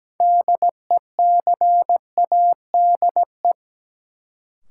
Morse code, heard as a keyed tone, about 17 wpm, 700 Hz. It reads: DECADE
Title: Sets of 5 Words - New Word Drill - 17wpm